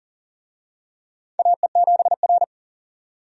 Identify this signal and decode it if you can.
Morse code, heard as a keyed tone, about 40 wpm, 700 Hz. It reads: AE7R
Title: USA Call Signs - 4 Characters - 40wpm